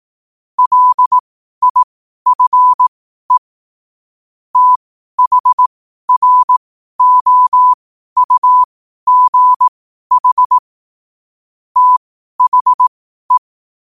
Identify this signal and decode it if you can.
Morse code, heard as a keyed tone, about 18 wpm, spaced as for 12 wpm, 1000 Hz. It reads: LIFE THROUGH THE
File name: qotdfiles / 12wpm